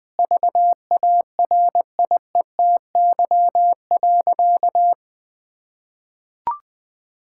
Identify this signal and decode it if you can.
Morse code, heard as a keyed tone, about 20 wpm, 700 Hz. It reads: VARIETY.
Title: Sentences with 6-Word Phrases - 20wpm